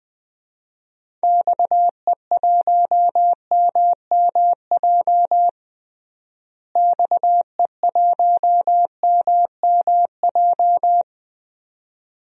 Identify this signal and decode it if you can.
Morse code, heard as a keyed tone, about 20 wpm, 700 Hz. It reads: XE1MMJ XE1MMJ
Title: DXCC-100 Call Signs - 6 Characters - Repeated 2x - 20wpm